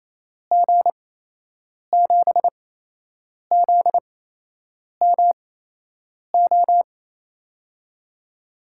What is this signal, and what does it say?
Morse code, heard as a keyed tone, about 28 wpm, spaced as for 8 wpm, 700 Hz. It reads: G7ZMO
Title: USA Call Signs - 6 Characters - 8x Character Spacing - 28wpm